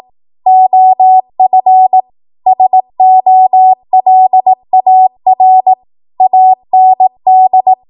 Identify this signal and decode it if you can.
Morse code, heard as a keyed tone, about 18 wpm, 750 Hz. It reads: OF SOLAR AND
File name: Morse / 18WPM